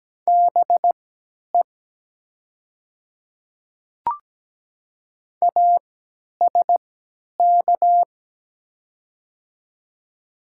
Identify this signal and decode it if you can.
Morse code, heard as a keyed tone, about 17 wpm, spaced as for 10 wpm, 700 Hz. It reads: BE ASK
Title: 3x Character Spacing - Top 200 Words - 17wpm